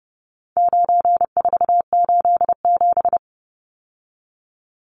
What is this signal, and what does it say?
Morse code, heard as a keyed tone, about 30 wpm, 700 Hz. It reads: 9487
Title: Numbers - 4-Digit 30wpm